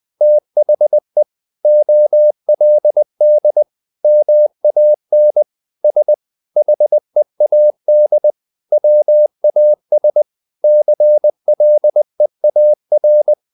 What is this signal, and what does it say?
Morse code, heard as a keyed tone, about 20 wpm, 600 Hz. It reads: THE OLD MAN S HEAD WAS CLEAR
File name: the old man and the sea part 2 20wpm